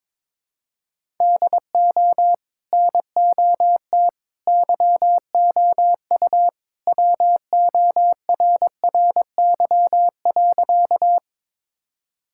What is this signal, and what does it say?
Morse code, heard as a keyed tone, about 21 wpm, 700 Hz. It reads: DO NOT YOU WORRY.
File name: practice / 22wpm / Sentences with 3-Word Phrases - 22wpm